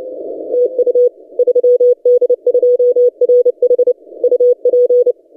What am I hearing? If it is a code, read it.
Morse code, heard as a keyed tone, about 30 wpm, 490 Hz. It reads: TU 3D2RH UP
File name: posts / 3d2rh-20cw- -0330z